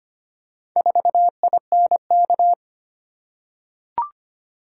Morse code, heard as a keyed tone, about 25 wpm, 700 Hz. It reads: 4INK
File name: practice / 25wpm / Four Letter-Number Combinations - 25wpm